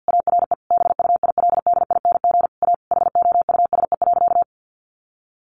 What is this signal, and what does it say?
Morse code, heard as a keyed tone, about 50 wpm, 700 Hz. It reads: ARE BUILDING A HOUSE.
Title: Sentences from Top 700 Words 50wpm